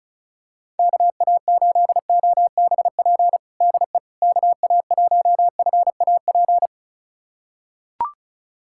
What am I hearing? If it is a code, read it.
Morse code, heard as a keyed tone, about 36 wpm, 700 Hz. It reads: KA8OBP DE KA1FAP
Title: USA Callsign DE Callsign - 6 Characters - 35wpm